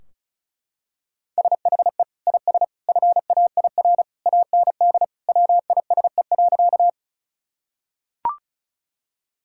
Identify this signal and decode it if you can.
Morse code, heard as a keyed tone, about 36 wpm, 700 Hz. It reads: SHE IS FAIR AND WISE.
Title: Sentences from Top 1600 Words 35wpm